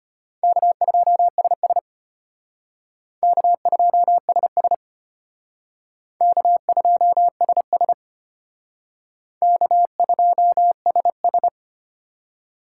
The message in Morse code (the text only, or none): K2SS K2SS K2SS K2SS